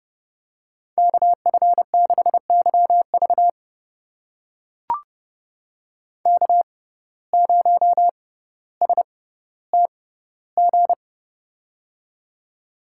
Morse code, heard as a keyed tone, about 30 wpm, spaced as for 15 wpm, 700 Hz. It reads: KF6YV K0STG
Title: USA Call Signs - 5 Characters - 6x Character Spacing - 30wpm